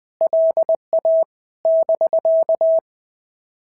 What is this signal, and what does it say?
Morse code, heard as a keyed tone, about 20 wpm, 650 Hz. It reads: LA <BK>
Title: POTA QSO Practice - 20wpm